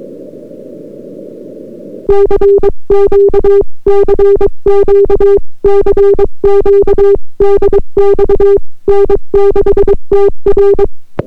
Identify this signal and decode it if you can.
Morse code, heard as a keyed tone, about 22 wpm, 400 Hz. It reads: CQ CQ CQ DX N6TR